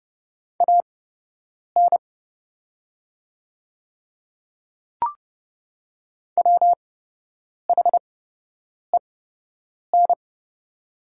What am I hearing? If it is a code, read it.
Morse code, heard as a keyed tone, about 30 wpm, spaced as for 8 wpm, 700 Hz. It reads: AN WHEN
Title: 8x Character Spacing - Top 100 Words - 30wpm